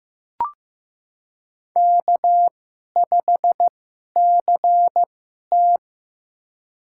Morse code, heard as a keyed tone, about 15 wpm, spaced as for 11 wpm, 700 Hz. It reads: K5CT
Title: USA Call Signs - 4 Characters - 2x Character Spacing - 15wpm